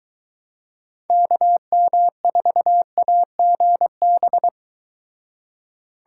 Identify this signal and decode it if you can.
Morse code, heard as a keyed tone, about 23 wpm, 700 Hz. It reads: KM4AGB